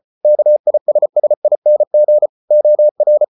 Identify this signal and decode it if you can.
Morse code, heard as a keyed tone, about 34 wpm, 600 Hz. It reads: KISSING OR